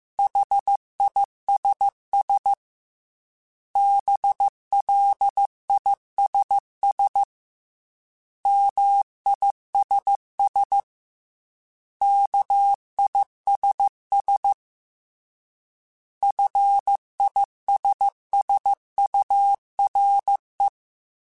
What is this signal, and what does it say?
Morse code, heard as a keyed tone, about 15 wpm, 800 Hz. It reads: HISS BLISS MISS KISS FISSURE